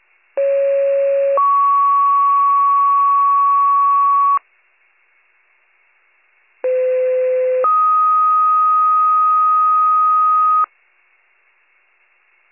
Two-tone page, heard as a one-second tone, then a three-second tone, about 550 then 1120 Hz, about 510 then 1250 Hz. It is heard from 0.4 to 10.6 s.